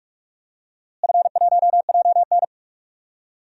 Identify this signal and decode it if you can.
Morse code, heard as a keyed tone, about 46 wpm, 700 Hz. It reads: F1JN
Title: DXCC-100 Call Signs - 4 Characters - Repeated 3x - 45wpm